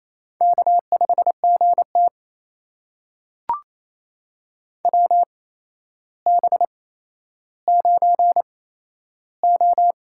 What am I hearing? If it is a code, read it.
Morse code, heard as a keyed tone, about 28 wpm, spaced as for 12 wpm, 700 Hz. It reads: K5GT WB9O
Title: USA Call Signs - 4 Characters - 8x Character Spacing - 28wpm